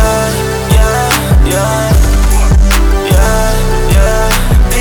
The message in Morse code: none